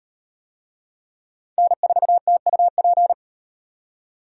Morse code, heard as a keyed tone, about 34 wpm, 700 Hz. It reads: N4TUP